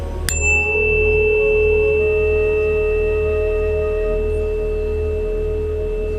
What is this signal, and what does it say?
Dial tone (Europe, 425 Hz), with noise.